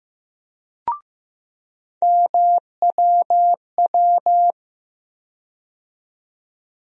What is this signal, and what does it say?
Morse code, heard as a keyed tone, about 15 wpm, 700 Hz. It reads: MWW
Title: USA Call Sign Postfixes - 3 Letters 15wpm